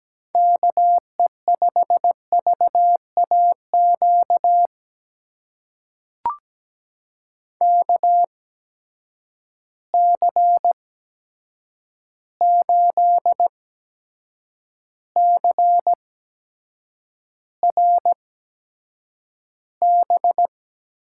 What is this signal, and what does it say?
Morse code, heard as a keyed tone, about 17 wpm, spaced as for 7 wpm, 700 Hz. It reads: KE5VAQ KC8CRB
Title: USA Call Signs - 8x Character Spacing - 17wpm